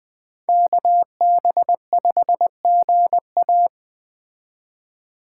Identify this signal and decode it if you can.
Morse code, heard as a keyed tone, about 20 wpm, 700 Hz. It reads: KB5GA
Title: K1USN SST QSO Practice - 20wpm